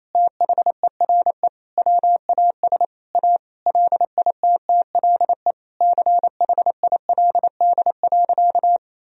Morse code, heard as a keyed tone, about 28 wpm, 700 Hz. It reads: THERE WAS A LITTLE CHILD.
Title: Sentences from Top 1000 Words - Review Part 1 - 28wpm